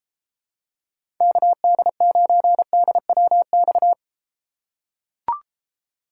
Morse code, heard as a keyed tone, about 32 wpm, 700 Hz. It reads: KD9DWX